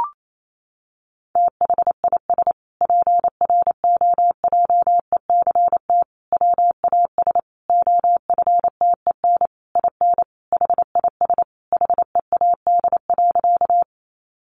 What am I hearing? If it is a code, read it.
Morse code, heard as a keyed tone, about 28 wpm, 700 Hz. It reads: THIS PROJECT WAS OFTEN IN HIS HEAD.
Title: Sentences from Top 1400 Words 28wpm